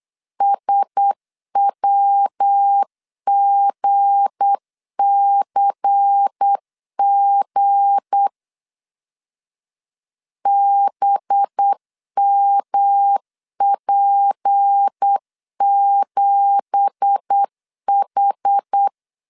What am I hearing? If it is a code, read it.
Morse code, heard as a keyed tone, about 8 wpm, 800 Hz. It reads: SWGCG BMP7H
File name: LEZIONI / 10 SIGNS / 10A SLASH 8wpm 3452